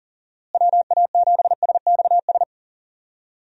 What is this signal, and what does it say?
Morse code, heard as a keyed tone, about 40 wpm, 700 Hz. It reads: WA7SXS
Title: USA Call Signs - 6 Characters - Repeated 3x - 40wpm